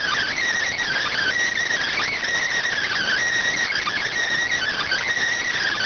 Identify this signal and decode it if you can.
Emergency Alert System (SAME) tones (header bursts), heard as harsh buzzing bursts. It is heard from 0.8 to 5.1 s.